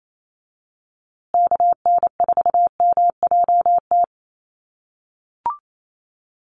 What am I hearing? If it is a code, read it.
Morse code, heard as a keyed tone, about 28 wpm, 700 Hz. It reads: KN4MJT
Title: USA Call Signs Repeated 3x 28wpm